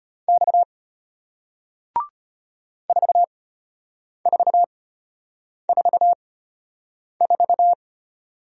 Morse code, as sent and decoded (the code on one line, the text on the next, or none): -..- / ....- ....- ....- ....-
X 4444